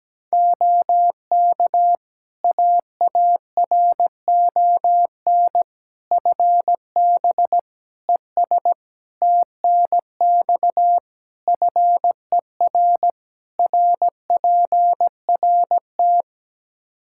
Morse code, heard as a keyed tone, about 17 wpm, 700 Hz. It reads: OK AARON FB ES TNX FER RPRT
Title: Practice QSO - Exchange 2 - Info - 17wpm